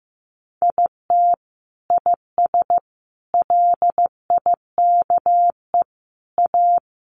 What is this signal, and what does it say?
Morse code, heard as a keyed tone, about 15 wpm, 700 Hz. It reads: IT IS LIKE A
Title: Sentences from Top 1500 Words 15wpm